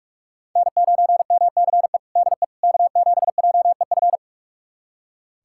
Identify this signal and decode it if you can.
Morse code, heard as a keyed tone, about 46 wpm, 700 Hz. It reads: N9MCE DE K7JEF
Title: USA Callsign DE Callsign - 5 Characters - 45wpm